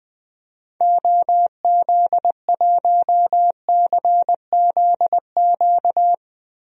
Morse code, heard as a keyed tone, about 20 wpm, 700 Hz. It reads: OZ1CZQ
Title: DXCC-100 Call Signs - 6 Characters - Repeated 2x - 20wpm